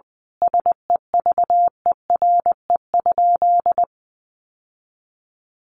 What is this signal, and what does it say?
Morse code, heard as a keyed tone, about 20 wpm, 700 Hz. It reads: SEVERE?
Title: Sentences from Top 1400 Words 20wpm